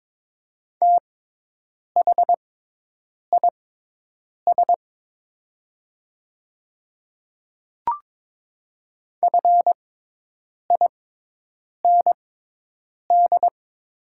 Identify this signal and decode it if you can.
Morse code, heard as a keyed tone, about 22 wpm, spaced as for 8 wpm, 700 Hz. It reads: THIS FIND
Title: 6x Character Spacing - Top 100 Words - 22wpm